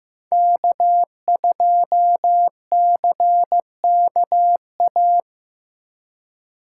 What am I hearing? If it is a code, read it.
Morse code, heard as a keyed tone, about 17 wpm, 700 Hz. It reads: K2CKA
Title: Speed Racing - USA Call Signs 15wpm